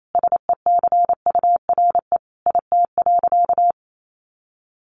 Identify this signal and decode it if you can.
Morse code, heard as a keyed tone, about 28 wpm, 700 Hz. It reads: SECURE IT.
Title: Sentences from Top 1200 Words 28wpm